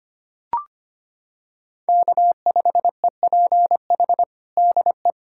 Morse code, heard as a keyed tone, about 25 wpm, 700 Hz. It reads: K5EPH DE
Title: USA Callsign DE Callsign 25wpm